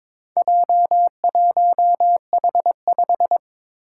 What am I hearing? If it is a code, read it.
Morse code, heard as a keyed tone, about 22 wpm, 700 Hz. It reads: J1H5
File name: practice / 22wpm / Four Letter-Number Combinations - 22wpm